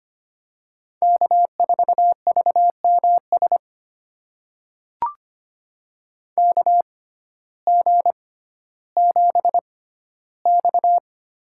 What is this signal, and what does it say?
Morse code, heard as a keyed tone, about 25 wpm, spaced as for 14 wpm, 700 Hz. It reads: K4VMS KG7X
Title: USA Call Signs - 5 Characters - 6x Character Spacing - 25wpm